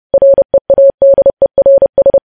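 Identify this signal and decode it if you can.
Morse code, heard as a keyed tone, about 30 wpm, 550 Hz. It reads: READERS
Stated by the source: Morse news headlines sent at 30 wpm.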